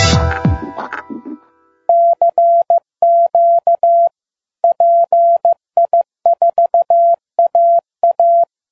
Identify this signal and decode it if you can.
Morse code, heard as a keyed tone, about 15 wpm, 680 Hz. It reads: CQ PI4AA